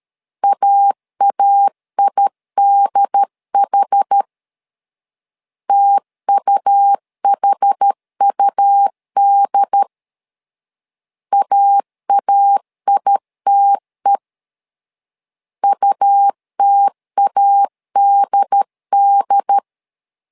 Morse code, heard as a keyed tone, about 13 wpm, 800 Hz. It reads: AAIDH TUHUD AAITE UTADD
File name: LEZIONI / 2 UHDN / 2 UHD 12wpm 1540